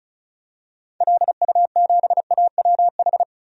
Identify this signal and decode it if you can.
Morse code, heard as a keyed tone, about 36 wpm, 700 Hz. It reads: LU7AWH
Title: DXCC-100 Call Signs - 6 Characters - 35wpm